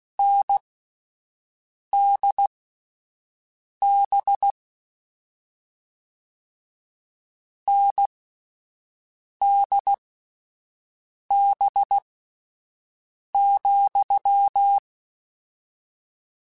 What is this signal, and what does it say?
Morse code, heard as a keyed tone, about 16 wpm, spaced as for 5 wpm, 800 Hz. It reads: NDB NDB,
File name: Difficult Combos / Difficult Combos-mp3s / NDB NDB6 @5